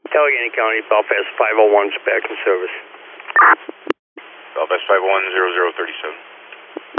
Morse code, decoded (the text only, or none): none